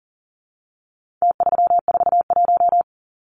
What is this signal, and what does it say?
Morse code, heard as a keyed tone, about 40 wpm, 700 Hz. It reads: T341
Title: Numbers - Contest Style - 4-Digit 40wpm